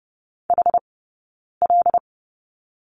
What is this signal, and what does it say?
Morse code, heard as a keyed tone, about 30 wpm, spaced as for 9 wpm, 690 Hz. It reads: HL
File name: Lesson 17d - Letter u - Two Characters 30wpm